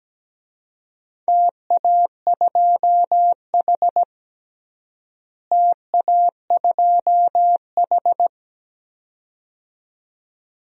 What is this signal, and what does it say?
Morse code, heard as a keyed tone, about 17 wpm, 700 Hz. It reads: TA2H TA2H
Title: DXCC-100 Call Signs - 4 Characters - Repeated 2x - 17wpm